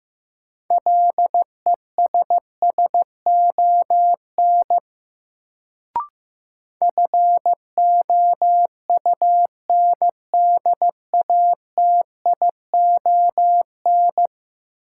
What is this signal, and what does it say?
Morse code, heard as a keyed tone, about 15 wpm, 700 Hz. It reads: LESSON FOUNDATION
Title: Sentences from Top 1200 Words - New Word Drill - 15wpm